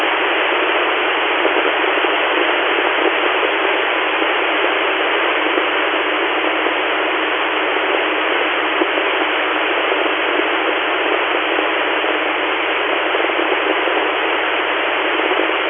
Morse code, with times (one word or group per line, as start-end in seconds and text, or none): none